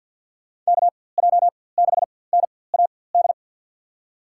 Morse code, heard as a keyed tone, about 48 wpm, spaced as for 23 wpm, 700 Hz. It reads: KJ6NAD
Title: USA Call Signs - 6 Characters - 4x Character Spacing - 50wpm